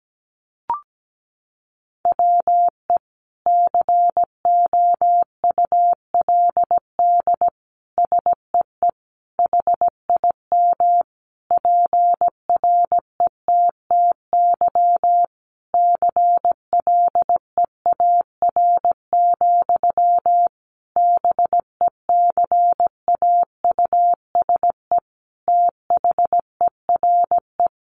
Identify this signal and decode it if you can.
Morse code, heard as a keyed tone, about 17 wpm, 700 Hz. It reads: WE COULD SEE HIM PRETTY CLEAR, BECAUSE THERE
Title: Sentences from Top 1000 Words 17wpm